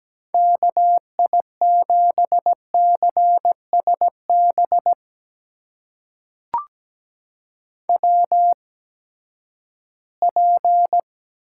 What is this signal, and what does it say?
Morse code, heard as a keyed tone, about 17 wpm, spaced as for 12 wpm, 700 Hz. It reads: KI7CSB WP
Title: USA Call Signs - 6 Characters - 8x Character Spacing - 17wpm